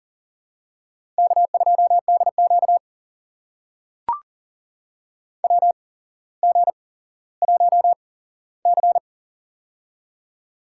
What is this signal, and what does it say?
Morse code, heard as a keyed tone, about 40 wpm, spaced as for 17 wpm, 700 Hz. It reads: K2DQ WG1C
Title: USA Call Signs - 8x Character Spacing - 40wpm